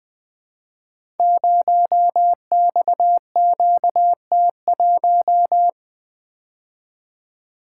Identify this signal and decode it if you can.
Morse code, heard as a keyed tone, about 20 wpm, 700 Hz. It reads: 0XQT1
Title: Code Groups - 5 Letter:Numbers - 20wpm